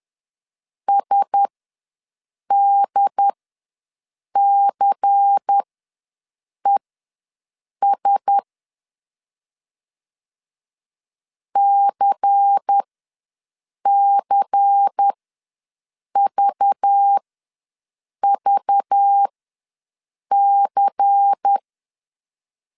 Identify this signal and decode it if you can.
Morse code, heard as a keyed tone, about 10 wpm, spaced as for 6 wpm, 800 Hz. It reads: SDCES CCVVC
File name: LEZIONI / 5 LBVC / 5 LBVC ws 10wpm 3733